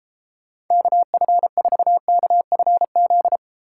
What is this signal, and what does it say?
Morse code, heard as a keyed tone, about 30 wpm, 700 Hz. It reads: KF4KFZ